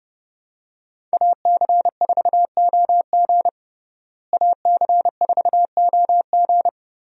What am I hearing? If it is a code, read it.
Morse code, heard as a keyed tone, about 30 wpm, 700 Hz. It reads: AC4OG AC4OG
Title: USA Call Signs - 5 Characters - Repeated 3x - 30wpm